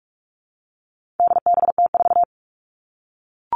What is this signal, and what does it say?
Morse code, heard as a keyed tone, about 44 wpm, 700 Hz. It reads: DBT4
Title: All Cut Numbers - 4-Digit 45wpm